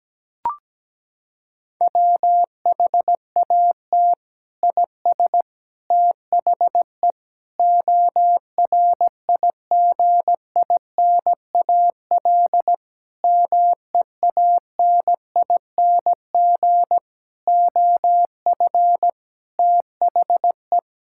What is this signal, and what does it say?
Morse code, heard as a keyed tone, about 17 wpm, 700 Hz. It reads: WHAT IS THE ORIGINAL MEANING OF THE
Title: Sentences from Top 1000 Words - Review Part 6 - 17wpm